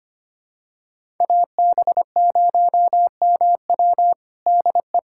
Morse code, heard as a keyed tone, about 25 wpm, 700 Hz. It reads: AB0MW DE
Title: USA Callsign DE Callsign - 5 Characters - 25wpm